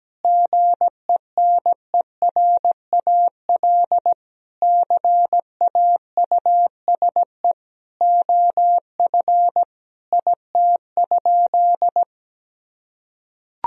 Morse code, heard as a keyed tone, about 17 wpm, 700 Hz. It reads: GENERAL CAUSE OF IT?
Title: Sentences from Top 1000 Words - Review Part 3 - 17wpm